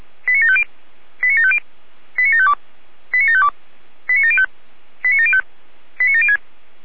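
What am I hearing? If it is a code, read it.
Five-tone selective call (ZVEI), heard as quick runs of five tones, tones of about 70 ms. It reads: 89750 89750 89752 89752 89705 89705 89706
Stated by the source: ZVEI five-tone selective call, code 89518, received with an SDR.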